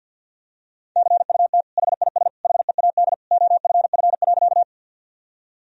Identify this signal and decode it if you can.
Morse code, heard as a keyed tone, about 52 wpm, 700 Hz. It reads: CUT HIS HEAD OFF.
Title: Sentences from Top 1000 Words - Review Part 1 - 50wpm